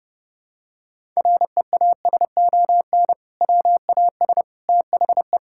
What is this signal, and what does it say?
Morse code, heard as a keyed tone, about 30 wpm, 700 Hz. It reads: REASON WAS THE
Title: Sentences from Top 1000 Words - Review Part 5 - 30wpm